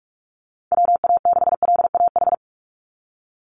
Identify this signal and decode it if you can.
Morse code, heard as a keyed tone, about 44 wpm, 700 Hz. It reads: WA6LAH
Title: USA Call Signs 45wpm